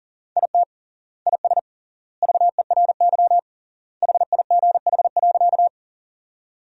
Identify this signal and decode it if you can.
Morse code, heard as a keyed tone, about 37 wpm, 700 Hz. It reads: IT IS VERY HIGH.
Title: 3x Word-Spacing - Sentences from Top 200 Words - 40wpm